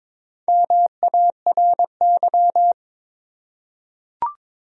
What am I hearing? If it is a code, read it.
Morse code, heard as a keyed tone, about 22 wpm, 700 Hz. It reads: MARY
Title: USA Common Names - 22wpm